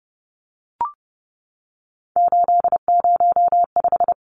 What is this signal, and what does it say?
Morse code, heard as a keyed tone, about 30 wpm, 700 Hz. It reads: 805
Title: Numbers - 3-Digit 30wpm